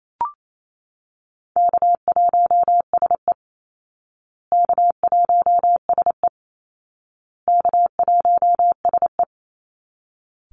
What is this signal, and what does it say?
Morse code, heard as a keyed tone, about 28 wpm, 700 Hz. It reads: K1SE K1SE K1SE
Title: Easy Call Signs Repeated 3x - 28wpm